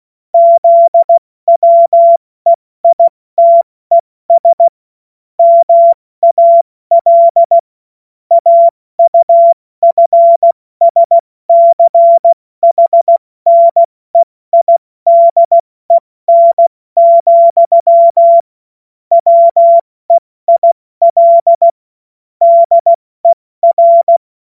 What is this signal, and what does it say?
Morse code, heard as a keyed tone, about 16 wpm, 680 Hz. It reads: ZWEITES MAL AUFSCHNEIDEN, WEIL DER
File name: Morsekurs / Witze / Witz 13 Februar 2024 70 F80